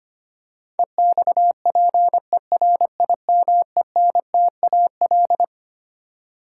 Morse code, heard as a keyed tone, about 25 wpm, 700 Hz. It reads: EXPERIMENTAL
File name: Twelve Letter Words - 25wpm